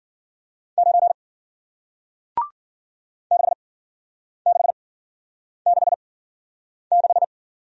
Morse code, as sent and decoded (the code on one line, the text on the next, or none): ----. / -.... -.... -.... -....
9 6666